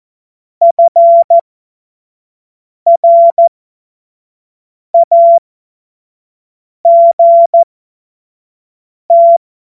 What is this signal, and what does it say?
Morse code, heard as a keyed tone, about 14 wpm, spaced as for 5 wpm, 680 Hz. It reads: FRAGT
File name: Morsekurs / Witze / Witz 06. Juli 2021, 25 F70